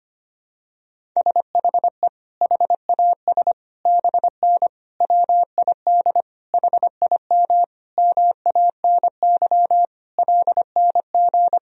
Morse code, heard as a keyed tone, about 25 wpm, 700 Hz. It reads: SHE HAS BN WID HIM MANY LNG
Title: Sentences from Top 200 Words and Abbreviations 25wpm